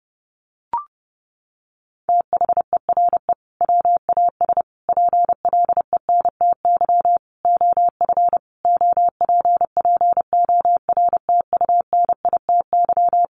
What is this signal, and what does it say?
Morse code, heard as a keyed tone, about 30 wpm, 700 Hz. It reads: THERE WAS PLENTY OF OPPORTUNITY
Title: Sentences from Top 1300 Words 30wpm